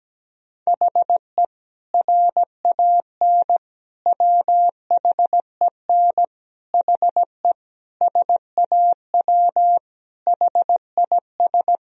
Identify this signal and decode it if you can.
Morse code, heard as a keyed tone, about 17 wpm, 700 Hz. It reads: HE RAN WHEN HE SAW HIS
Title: Sentences from Top 500 Words and Abbreviations 17wpm